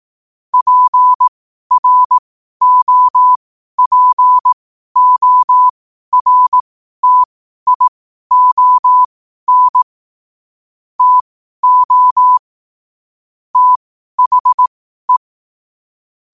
Morse code, heard as a keyed tone, about 18 wpm, spaced as for 12 wpm, 1000 Hz. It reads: PROPORTION TO THE
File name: qotdfiles / 12wpm